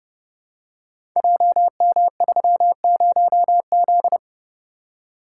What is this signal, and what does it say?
Morse code, heard as a keyed tone, about 31 wpm, 700 Hz. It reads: JM30Z
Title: Code Groups - 5 Letter:Numbers - 30wpm